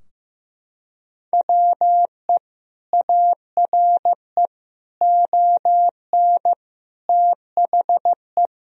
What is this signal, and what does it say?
Morse code, heard as a keyed tone, about 15 wpm, 700 Hz. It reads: WE ARE ON THE
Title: Sentences from Top 1000 Words - Review Part 3 - 15wpm